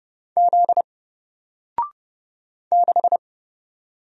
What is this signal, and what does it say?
Morse code, heard as a keyed tone, about 31 wpm, 700 Hz. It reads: Z 6
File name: Single-Letter-Number-30wpm